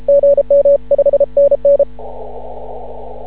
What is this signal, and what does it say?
Morse code, heard as a keyed tone, about 31 wpm, 560 Hz. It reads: GM5NN
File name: CW / ha6oi 2